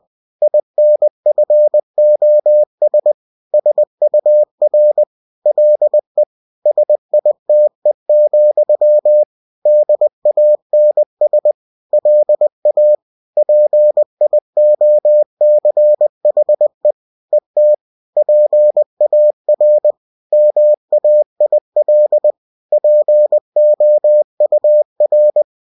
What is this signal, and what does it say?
Morse code, heard as a keyed tone, about 20 wpm, 600 Hz. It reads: INFOS SUR LE SITE, DANS LA PIOCHE ET PAR MAIL POUR